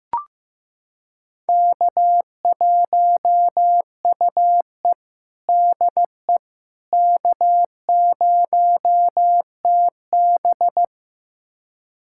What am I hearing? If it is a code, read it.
Morse code, heard as a keyed tone, about 15 wpm, 700 Hz. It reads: K1UE DE K0TB
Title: USA Callsign DE Callsign - 4 Characters - 15wpm